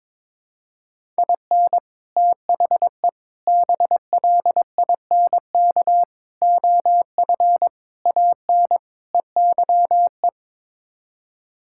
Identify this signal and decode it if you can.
Morse code, heard as a keyed tone, about 22 wpm, 700 Hz. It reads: IN THE BLINK OF AN EYE
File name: practice / 22wpm / Sets of 6 Words - 22wpm